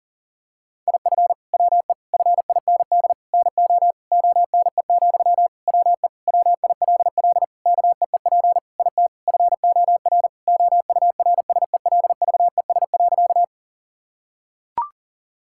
Morse code, heard as a keyed tone, about 40 wpm, 700 Hz. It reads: IF WE FIND NO ONE, WE WILL KEEP IT FOR OURSELVES.
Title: Sentences from Top 1600 Words 40wpm